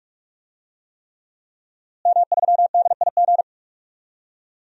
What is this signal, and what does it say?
Morse code, heard as a keyed tone, about 41 wpm, 700 Hz. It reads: M3DIG
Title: DXCC-100 Call Signs - 5 Characters - Repeated 3x - 45wpm